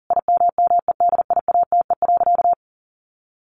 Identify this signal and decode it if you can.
Morse code, heard as a keyed tone, about 40 wpm, 700 Hz. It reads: IMMEDIATE.
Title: Sentences from Top 1300 Words 40wpm